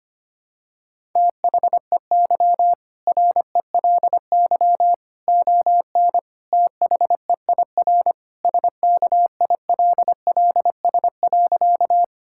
Morse code, heard as a keyed tone, about 25 wpm, 700 Hz. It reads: THEY RELY ON THEIR SKILLS.
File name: practice / 25wpm / Sentences with 2-Word Phrases - 25wpm